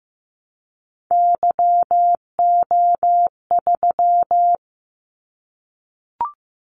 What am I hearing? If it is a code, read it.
Morse code, heard as a keyed tone, about 15 wpm, 700 Hz. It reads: YO3
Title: Callsign Prefixes for Top 25 DXCC - 15wpm